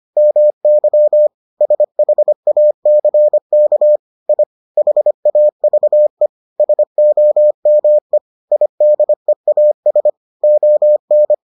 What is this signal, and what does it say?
Morse code, heard as a keyed tone, about 25 wpm, 600 Hz. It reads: MY SHACK I HAVE SOME IDEAS ON